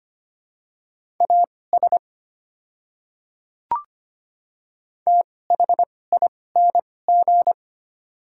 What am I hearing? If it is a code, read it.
Morse code, heard as a keyed tone, about 25 wpm, spaced as for 18 wpm, 690 Hz. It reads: AS THING
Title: 2x Character Spacing - Top 100 Words - 25wpm